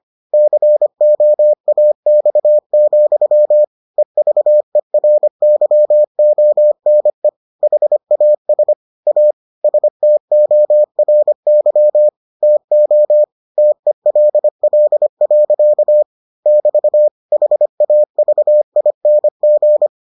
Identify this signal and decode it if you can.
Morse code, heard as a keyed tone, about 25 wpm, 600 Hz. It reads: COAX, EVERYONE HAS A STORY TO TELL. <BT> HAVING